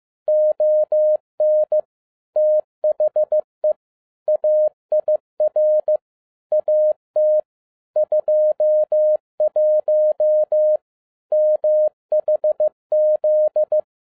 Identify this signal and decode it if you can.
Morse code, heard as a keyed tone, about 15 wpm, 610 Hz. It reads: ON THE AIR AT 21 MHZ